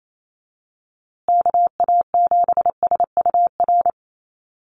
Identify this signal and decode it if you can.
Morse code, heard as a keyed tone, about 28 wpm, 700 Hz. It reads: KA7SUR
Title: USA Call Signs Repeated 2x 28wpm